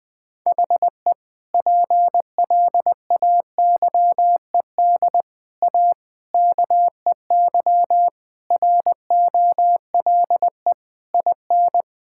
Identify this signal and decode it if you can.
Morse code, heard as a keyed tone, about 20 wpm, 700 Hz. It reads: HE PLAYED A KEY ROLE IN
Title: Sentences with 6-Word Phrases - 20wpm